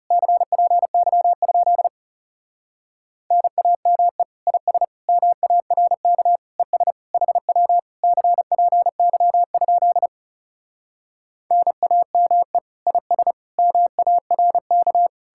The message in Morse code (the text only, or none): CPY? NAME IS MARK ES HW CPY? NAME IS MARK